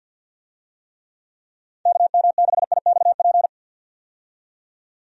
Morse code, heard as a keyed tone, about 46 wpm, 700 Hz. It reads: KM6IXP
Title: USA Call Signs - 6 Characters - 50wpm